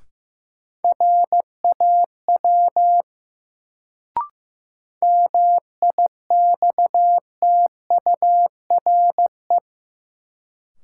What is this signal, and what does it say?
Morse code, heard as a keyed tone, about 15 wpm, 700 Hz. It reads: RAW MIXTURE
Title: Sentences from Top 1500 Words - New Word Drill Alt - 15wpm